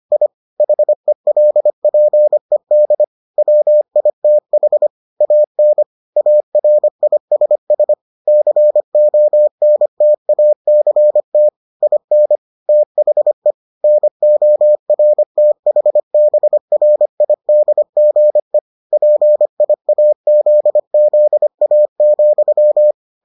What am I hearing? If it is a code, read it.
Morse code, heard as a keyed tone, about 25 wpm, 600 Hz. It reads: I HELPED WITH AN ARISS CONTACT IN THE NORTHBRIDGE PIAZZA,